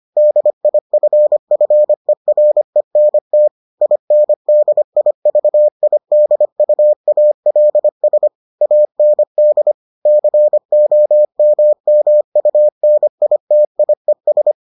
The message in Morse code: DIFFERENT INDIVIDUALS AND COMMUNITIES

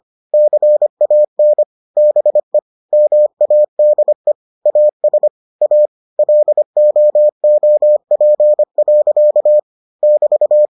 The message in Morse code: CAN BE MADE AS A LOOP. <BT>

